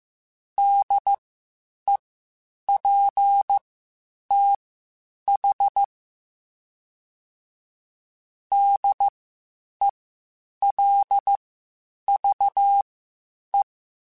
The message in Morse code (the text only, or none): DEPTH DELVE